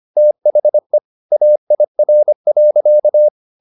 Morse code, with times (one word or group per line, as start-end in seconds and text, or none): 0.2-1.0 THE
1.3-3.3 AIR.